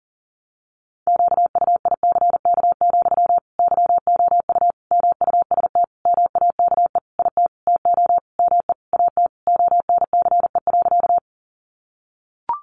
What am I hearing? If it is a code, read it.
Morse code, heard as a keyed tone, about 40 wpm, 700 Hz. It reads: QUICK, YOU MUST MAKE IT TO ME AT ONCE.
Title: Sentences from Top 500 Words 40wpm